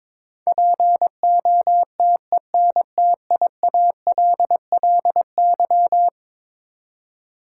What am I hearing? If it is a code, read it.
Morse code, heard as a keyed tone, about 22 wpm, 700 Hz. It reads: POTENTIALLY
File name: practice / 22wpm / Eleven Letter Words - 22wpm